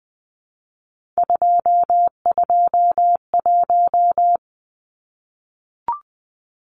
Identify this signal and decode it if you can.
Morse code, heard as a keyed tone, about 20 wpm, 700 Hz. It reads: 221
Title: Numbers - 3-Digit 20wpm